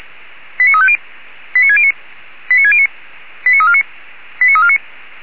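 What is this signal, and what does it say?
Five-tone selective call (ZVEI), heard as quick runs of five tones, tones of about 70 ms. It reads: 89350 89609 89609 89349 89349
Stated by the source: ZVEI five-tone selective call 89293 received on an SDR, from the Feuerwehr Orlamünde audio archive.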